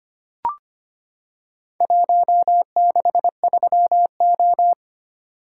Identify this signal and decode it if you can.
Morse code, heard as a keyed tone, about 25 wpm, 700 Hz. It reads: 163O
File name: practice / 25wpm / Four Letter-Number Combinations - 25wpm